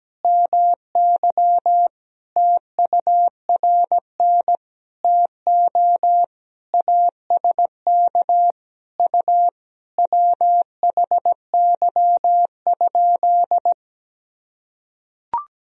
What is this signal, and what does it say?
Morse code, heard as a keyed tone, about 17 wpm, 700 Hz. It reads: MY TURN TO ASK U WHY?
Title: Sentences from Top 200 Words and Abbreviations 17wpm